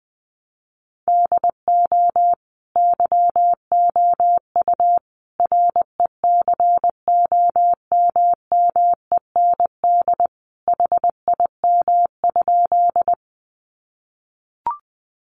Morse code, heard as a keyed tone, about 19 wpm, 700 Hz. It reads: DO YOU RECOMMEND HIM?
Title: Sentences from Top 1200 Words 20wpm